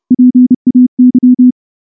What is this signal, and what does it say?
Morse code, heard as a keyed tone, about 30 wpm, 250 Hz. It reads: PAY
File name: morse / morse-training-30WPM